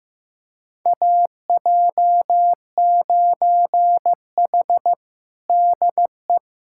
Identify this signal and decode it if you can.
Morse code, heard as a keyed tone, about 15 wpm, 700 Hz. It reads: AJ9H DE
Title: USA Callsign DE Callsign - 4 Characters - 15wpm